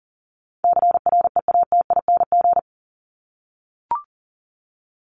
Morse code, heard as a keyed tone, about 40 wpm, 700 Hz. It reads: CREATING
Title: ING Suffix - Top 100 Words with Base Words 40wpm